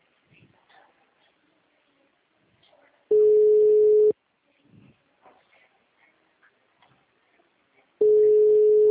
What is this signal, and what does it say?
Ringback tone (Europe, 425 Hz).
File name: out- -2017- - - .40272